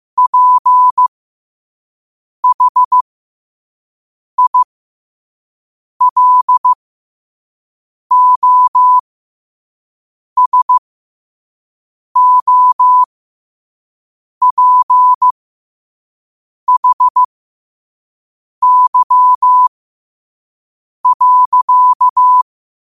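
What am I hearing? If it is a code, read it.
Morse code, heard as a keyed tone, about 15 wpm, spaced as for 5 wpm, 1000 Hz. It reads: PHILOSOPHY.
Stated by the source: Morse code quote of the day sent at 5 WPM.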